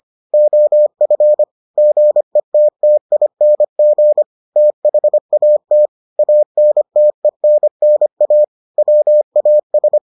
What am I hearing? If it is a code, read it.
Morse code, heard as a keyed tone, about 25 wpm, 600 Hz. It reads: OF GETTING THAT ANTENNA WAS